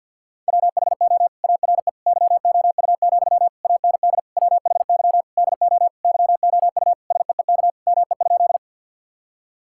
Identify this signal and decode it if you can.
Morse code, heard as a keyed tone, about 52 wpm, 700 Hz. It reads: WHO ARE YOU, AND WHY DO YOU SEEK ME?
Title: Sentences from Top 1600 Words 50wpm